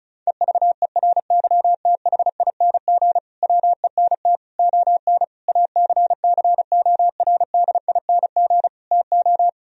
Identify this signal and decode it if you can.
Morse code, heard as a keyed tone, about 36 wpm, 700 Hz. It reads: EVERYTHING WENT ON ACCORDING TO
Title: Sentences from Top 1000 Words - Review Part 4 - 35wpm